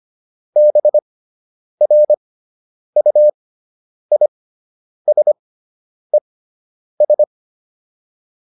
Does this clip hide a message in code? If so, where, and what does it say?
Yes: Morse code from 0.6 to 7.2 s, heard as a keyed tone, about 25 wpm, spaced as for 9 wpm, 600 Hz. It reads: BRUISES